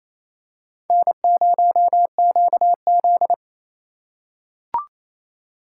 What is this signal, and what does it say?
Morse code, heard as a keyed tone, about 28 wpm, 700 Hz. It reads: N0QZ
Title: USA Call Signs - 4 Characters - 3x Character Spacing - 28wpm